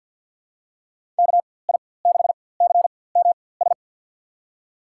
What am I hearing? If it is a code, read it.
Morse code, heard as a keyed tone, about 52 wpm, spaced as for 24 wpm, 700 Hz. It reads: KI6CMS
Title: USA Call Signs - 6 Characters - 4x Character Spacing - 50wpm